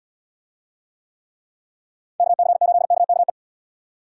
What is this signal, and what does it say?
Morse code, heard as a keyed tone, about 69 wpm, 700 Hz. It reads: KC8KKE